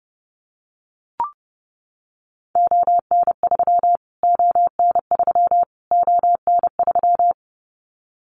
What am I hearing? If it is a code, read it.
Morse code, heard as a keyed tone, about 30 wpm, 700 Hz. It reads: ON3 ON3 ON3
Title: Callsign Prefixes for Top 25 DXCC - Repeated 3x - 30wpm